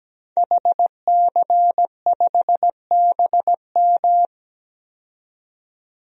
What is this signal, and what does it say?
Morse code, heard as a keyed tone, about 17 wpm, 700 Hz. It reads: HC5BM
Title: DXCC-100 Call Signs - 5 Characters - 17wpm